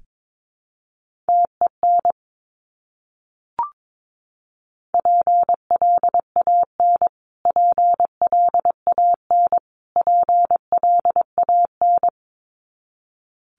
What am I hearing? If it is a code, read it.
Morse code, heard as a keyed tone, about 22 wpm, 700 Hz. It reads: TEN PLAN PLAN PLAN
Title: Top 500 Words Repeated 22wpm